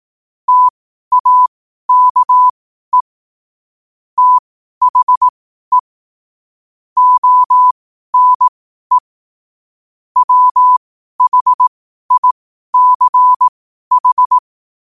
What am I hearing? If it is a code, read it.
Morse code, heard as a keyed tone, about 18 wpm, spaced as for 12 wpm, 1000 Hz. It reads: TAKE THE ONE WHICH
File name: qotdfiles / 12wpm